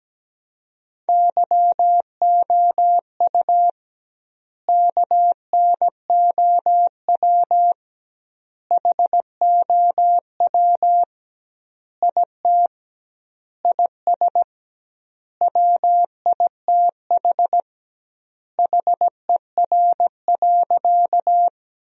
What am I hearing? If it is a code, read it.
Morse code, heard as a keyed tone, about 17 wpm, 700 Hz. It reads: YOU KNOW HOW IT IS WITH HER.
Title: 2x Word-Spacing - Sentences from Top 100 Words - 17wpm